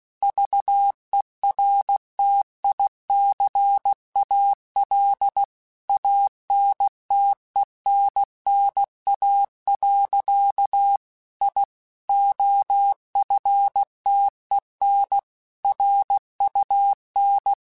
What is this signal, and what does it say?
Morse code, heard as a keyed tone, about 16 wpm, 800 Hz. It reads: VERTICAL ANTENNA. I OFTEN RUN
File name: QSOs / QSOs-mp3s / kc7sac@15